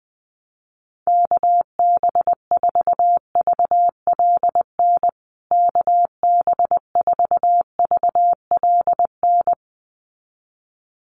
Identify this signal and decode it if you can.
Morse code, heard as a keyed tone, about 19 wpm, 700 Hz. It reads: KB4VLN KB4VLN
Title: USA Call Signs Repeated 2x 20wpm